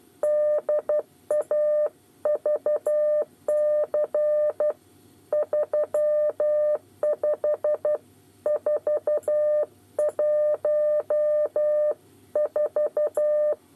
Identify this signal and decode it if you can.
Morse code, heard as a keyed tone, about 12 wpm, 600 Hz. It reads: DAVC 35414
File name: 8 bis lezione cw 2